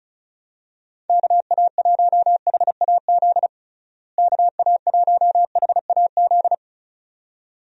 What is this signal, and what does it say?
Morse code, heard as a keyed tone, about 36 wpm, 700 Hz. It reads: KA1HAZ KA1HAZ
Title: USA Call Signs - 6 Characters - Repeated 2x - 35wpm